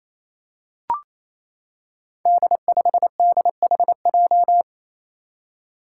Morse code, heard as a keyed tone, about 27 wpm, 700 Hz. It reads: D5DHJ
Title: Five Challenging Letter-Number Combinations 28wpm